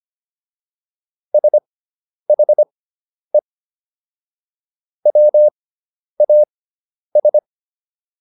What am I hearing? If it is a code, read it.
Morse code, heard as a keyed tone, about 25 wpm, spaced as for 10 wpm, 600 Hz. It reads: SHE WAS